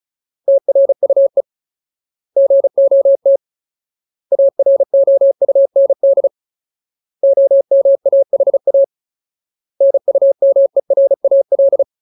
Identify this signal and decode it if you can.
Morse code, heard as a keyed tone, about 34 wpm, 550 Hz. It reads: TRUE GOT AROUND OMAHA NUMERAL
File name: Ultimate Word List / 35WPM Ultimate